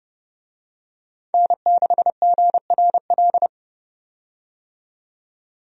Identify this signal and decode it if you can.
Morse code, heard as a keyed tone, about 30 wpm, 700 Hz. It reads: N6GRL
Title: USA Call Signs - 5 Characters - 30wpm